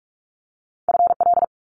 Morse code, heard as a keyed tone, about 46 wpm, 700 Hz. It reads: FL
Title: Multi-letter Phonemes Quick Fire 45wpm